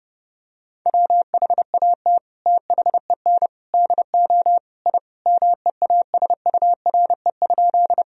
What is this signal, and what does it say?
Morse code, heard as a keyed tone, about 30 wpm, 700 Hz. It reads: WHAT THEN DO I MEASURE?
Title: Sentences from Top 1000 Words - Review Part 2 - 30wpm